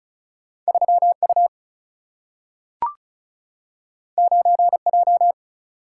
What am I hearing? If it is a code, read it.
Morse code, heard as a keyed tone, about 36 wpm, 700 Hz. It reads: 3U 9J
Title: Single-Letter Single-Number 35wpm